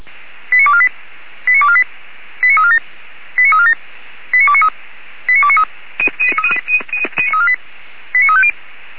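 Five-tone selective call (ZVEI), heard as quick runs of five tones, tones of about 70 ms. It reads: 89248 89248 89347 89347 89293 89293 89350
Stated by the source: ZVEI five-tone call received by SDR.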